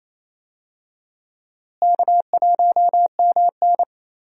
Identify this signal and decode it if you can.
Morse code, heard as a keyed tone, about 28 wpm, 700 Hz. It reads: K1MN
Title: USA Call Signs - 4 Characters - Repeated 3x - 28wpm